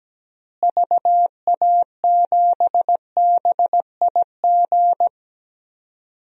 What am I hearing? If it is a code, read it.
Morse code, heard as a keyed tone, about 17 wpm, 700 Hz. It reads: VA7BIG
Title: DXCC-100 Call Signs - 6 Characters - Repeated 3x - 17wpm